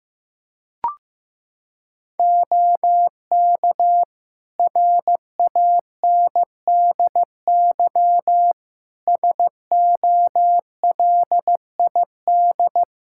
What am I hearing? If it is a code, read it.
Morse code, heard as a keyed tone, about 15 wpm, 700 Hz. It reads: OK RANDY SOLID
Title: Practice QSO - Exchange 3 - More Info - 15wpm